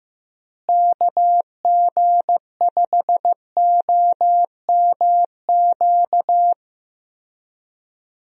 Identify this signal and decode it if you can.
Morse code, heard as a keyed tone, about 16 wpm, 700 Hz. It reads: KG5OMQ
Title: USA Call Signs - 6 Characters - Repeated 2x - 15wpm